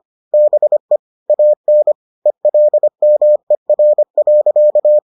Morse code, heard as a keyed tone, about 25 wpm, 600 Hz. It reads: BE AN ELMER.